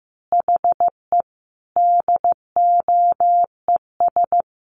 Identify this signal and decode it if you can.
Morse code, heard as a keyed tone, about 15 wpm, 700 Hz. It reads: HE DOES
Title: Additional Easy Phrases 15wpm